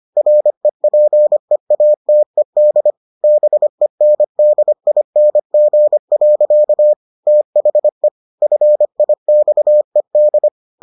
Morse code, heard as a keyed tone, about 25 wpm, 600 Hz. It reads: REPEATED BENDING. THE FIXED